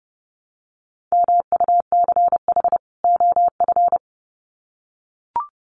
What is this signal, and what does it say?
Morse code, heard as a keyed tone, about 30 wpm, 700 Hz. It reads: MUCH OF
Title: Sets of 2 Words 30wpm